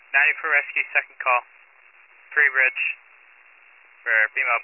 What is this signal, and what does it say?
Emergency Alert System (SAME) tones, heard as harsh buzzing bursts, with speech. It is heard from 0.4 to 4.4 s.